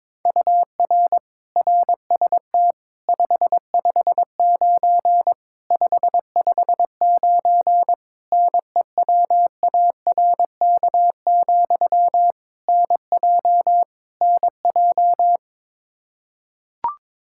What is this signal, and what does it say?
Morse code, heard as a keyed tone, about 22 wpm, 700 Hz. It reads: UR RST 559 559 NEWARK, NJ NJ
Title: Practice QSO - Exchange 1 - Beginning - 22wpm